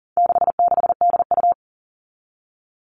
Morse code, heard as a keyed tone, about 40 wpm, 700 Hz. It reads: 66DU
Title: All Cut Numbers - 4-Digit 40wpm